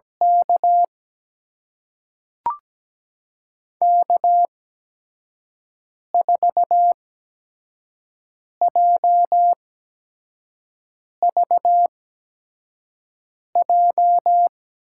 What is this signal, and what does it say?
Morse code, heard as a keyed tone, about 17 wpm, spaced as for 5 wpm, 700 Hz. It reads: K K4JVJ
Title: USA Call Signs - 5 Characters - 8x Character Spacing - 17wpm